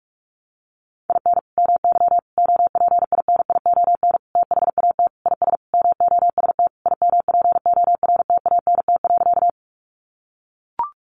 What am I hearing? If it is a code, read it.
Morse code, heard as a keyed tone, about 44 wpm, 700 Hz. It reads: IN MY OPINION THAT IS MOST IMPORTANT.
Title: Sentences from Top 1200 Words 45wpm